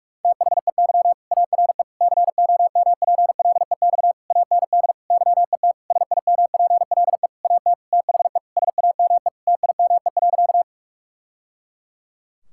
Morse code, heard as a keyed tone, about 46 wpm, 700 Hz. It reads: THEY ARE COMPLEX AND YET SIMPLE AT THE SAME TIME.
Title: Sentences from Top 1000 Words - Review Part 6 - 45wpm